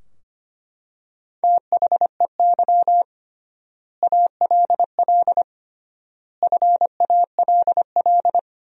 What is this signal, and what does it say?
Morse code, heard as a keyed tone, about 25 wpm, 700 Hz. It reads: THEY ALL FALL
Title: 3x Word-Spacing - Sentences from Top 500 Words - 25wpm